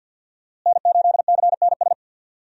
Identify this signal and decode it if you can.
Morse code, heard as a keyed tone, about 52 wpm, 700 Hz. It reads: N8CNS
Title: USA Call Signs - 5 Characters - Repeated 3x - 50wpm